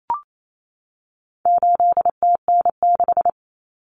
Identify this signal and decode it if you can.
Morse code, heard as a keyed tone, about 28 wpm, 700 Hz. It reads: 8TN6
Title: Common Cut Numbers - 4-Digit 28wpm